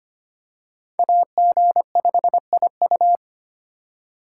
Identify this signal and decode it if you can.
Morse code, heard as a keyed tone, about 25 wpm, 700 Hz. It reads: AG5IU
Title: USA Call Signs - 5 Characters - 4x Character Spacing - 25wpm